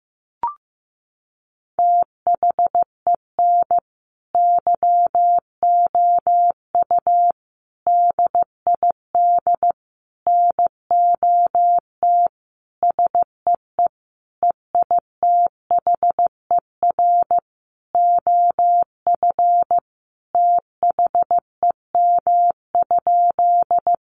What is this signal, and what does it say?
Morse code, heard as a keyed tone, about 15 wpm, 700 Hz. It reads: THEN YOU DID NOT SEE EITHER OF THEM?
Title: Sentences from Top 600 Words 15wpm